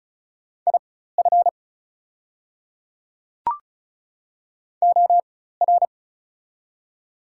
Morse code, heard as a keyed tone, about 36 wpm, spaced as for 16 wpm, 700 Hz. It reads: IF OR
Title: 4x Character Spacing - Top 100 Words - 35wpm